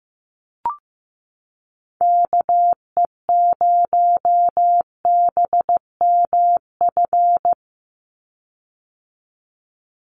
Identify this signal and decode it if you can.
Morse code, heard as a keyed tone, about 15 wpm, 700 Hz. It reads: KE0BMF
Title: USA Call Signs 15wpm